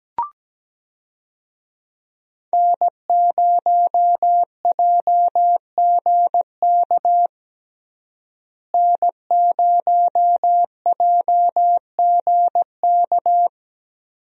Morse code, heard as a keyed tone, about 17 wpm, 700 Hz. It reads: N0JGK N0JGK
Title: USA Call Signs - 5 Characters - Repeated 2x - 17wpm